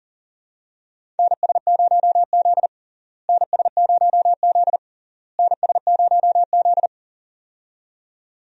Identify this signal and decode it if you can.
Morse code, heard as a keyed tone, about 40 wpm, 700 Hz. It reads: NS0Z NS0Z NS0Z
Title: USA Call Signs - 4 Characters - Repeated 3x - 40wpm